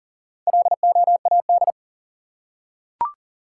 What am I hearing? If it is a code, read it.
Morse code, heard as a keyed tone, about 40 wpm, 700 Hz. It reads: LOAD
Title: Four Letter Words - 40wpm